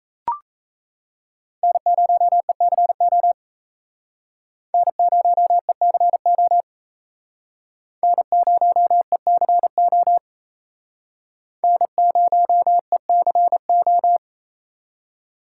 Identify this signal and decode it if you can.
Morse code, heard as a keyed tone, about 35 wpm, 700 Hz. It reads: N0ECO N0ECO N0ECO N0ECO
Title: Speed Racing - USA 1x3 Call Signs 28wpm